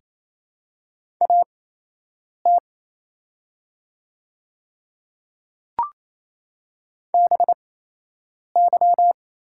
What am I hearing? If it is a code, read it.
Morse code, heard as a keyed tone, about 27 wpm, spaced as for 8 wpm, 690 Hz. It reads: AT BY
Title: 8x Character Spacing - Top 300 Words - 28wpm